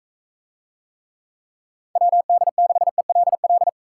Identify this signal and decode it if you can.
Morse code, heard as a keyed tone, about 42 wpm, 700 Hz. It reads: WD6ELL